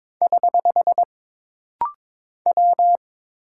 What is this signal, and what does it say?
Morse code, heard as a keyed tone, about 22 wpm, 700 Hz. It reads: <HH> W
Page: This rapid-fire timing is geared to push your ICR (Instant Character Recognition) skills.